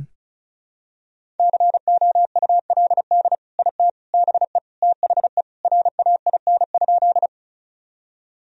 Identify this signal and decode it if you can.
Morse code, heard as a keyed tone, about 34 wpm, 700 Hz. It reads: COULD IT BE THE RAIN?
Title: Sentences from Top 1000 Words - Review Part 1 - 35wpm